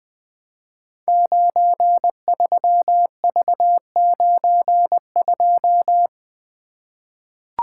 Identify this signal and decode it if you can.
Morse code, heard as a keyed tone, about 20 wpm, 700 Hz. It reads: 93V92
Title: Code Groups - 5 Letter:Numbers - 20wpm